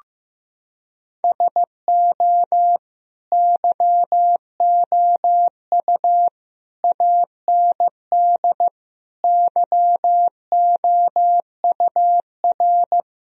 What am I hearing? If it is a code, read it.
Morse code, heard as a keyed tone, about 15 wpm, 700 Hz. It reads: SO YOU AND YOUR
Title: Sentences from Top 1000 Words - Review Part 4 - 15wpm